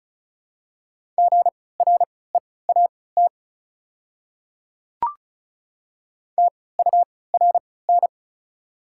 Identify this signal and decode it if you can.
Morse code, heard as a keyed tone, about 36 wpm, spaced as for 20 wpm, 690 Hz. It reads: GREAT TURN
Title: 3x Character Spacing - Top 300 Words - 35wpm